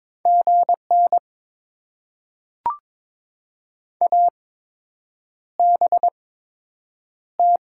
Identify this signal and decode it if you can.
Morse code, heard as a keyed tone, about 22 wpm, spaced as for 8 wpm, 700 Hz. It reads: GN ABT
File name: Top 90 QSO Elements - 8x Character Spacing - 22wpm